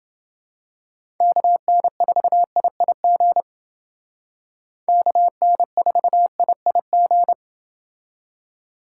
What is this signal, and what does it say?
Morse code, heard as a keyed tone, about 29 wpm, 700 Hz. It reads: KN4IIG KN4IIG